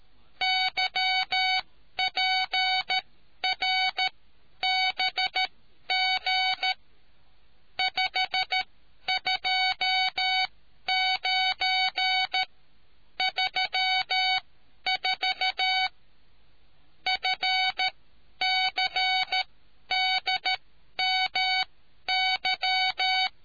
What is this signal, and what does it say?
Morse code, heard as a keyed tone, about 13 wpm, spaced as for 10 wpm, 750 Hz. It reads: YPRBG 52934 FCDMY